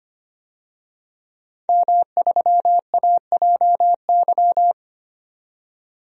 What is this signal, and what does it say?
Morse code, heard as a keyed tone, about 25 wpm, 700 Hz. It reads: M3AJY
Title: DXCC-100 Call Signs - 5 Characters - 25wpm